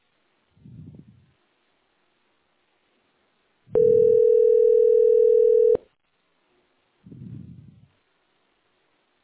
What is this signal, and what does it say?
Ringback tone (North America), 440 + 480 Hz.
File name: 9c896d41-5495-4931-bf80